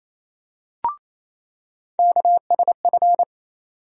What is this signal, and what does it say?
Morse code, heard as a keyed tone, about 28 wpm, 700 Hz. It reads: KSF
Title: USA Call Sign Postfixes - 3 Letters 28wpm